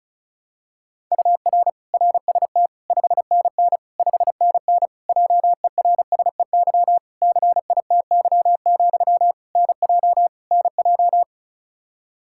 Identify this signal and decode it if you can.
Morse code, heard as a keyed tone, about 36 wpm, 700 Hz. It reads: UR RST 5NN 5NN JERSEY CITY, NJ NJ
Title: Practice QSO - Exchange 1 - Beginning - 35wpm